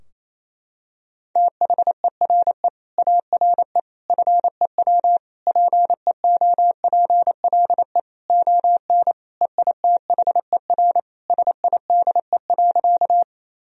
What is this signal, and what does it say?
Morse code, heard as a keyed tone, about 27 wpm, 700 Hz. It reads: THERE ARE FEW PEOPLE ON EITHER SIDE.
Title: Sentences from Top 1000 Words - Review Part 4 - 28wpm